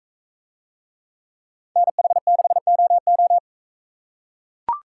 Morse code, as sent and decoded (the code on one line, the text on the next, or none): -. .... -.... --- ---
NH6OO